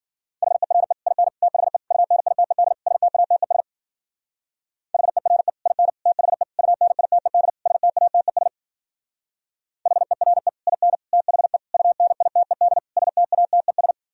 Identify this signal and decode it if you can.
Morse code, heard as a keyed tone, about 52 wpm, 700 Hz. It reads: HERE IN THE UNITED STATES HERE IN THE UNITED STATES HERE IN THE UNITED STATES